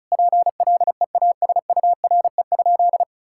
Morse code, heard as a keyed tone, about 36 wpm, 700 Hz. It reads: PLEASURE?
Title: Sentences from Top 1600 Words 35wpm